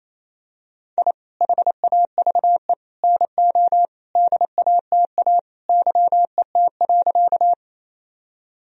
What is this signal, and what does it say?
Morse code, heard as a keyed tone, about 28 wpm, 700 Hz. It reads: I HAVE NO DATA YET.
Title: Sentences from Top 1000 Words - Review Part 3 - 28wpm